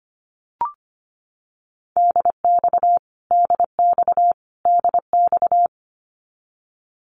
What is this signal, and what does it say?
Morse code, heard as a keyed tone, about 25 wpm, 700 Hz. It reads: DX DX DX
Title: Abbreviations Repeated 3x - 25wpm